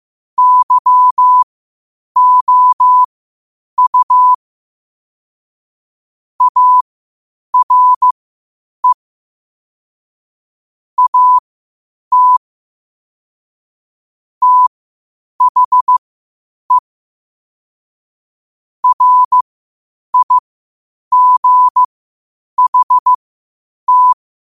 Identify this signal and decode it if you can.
Morse code, heard as a keyed tone, about 15 wpm, spaced as for 8 wpm, 1000 Hz. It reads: YOU ARE AT THE RIGHT
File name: qotdfiles / 8wpm